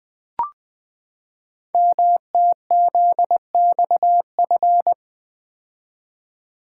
Morse code, heard as a keyed tone, about 20 wpm, 700 Hz. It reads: MTZXF
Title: Code Groups - 5 Letters - 20wpm